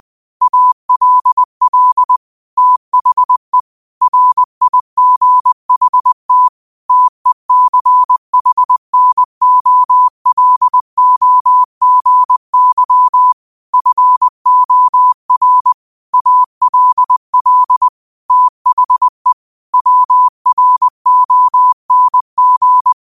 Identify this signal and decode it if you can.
Morse code, heard as a keyed tone, about 20 wpm, 1000 Hz. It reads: ALL THE RIGHT TECHNOLOGY FOR ALL THE WRONG